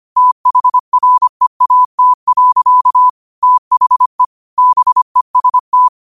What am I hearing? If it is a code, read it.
Morse code, heard as a keyed tone, about 25 wpm, 1000 Hz. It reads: THREAT. THE BEST
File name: 25wpm